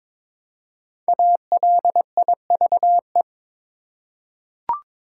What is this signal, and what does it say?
Morse code, heard as a keyed tone, about 22 wpm, 690 Hz. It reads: ALIVE
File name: 5 Letter Words - First 3 Letters Given - 22wpm